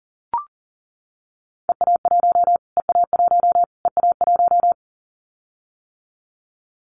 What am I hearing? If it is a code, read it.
Morse code, heard as a keyed tone, about 40 wpm, 700 Hz. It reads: EA1 EA1 EA1
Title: Callsign Prefixes for Top 25 DXCC - Repeated 3x - 40wpm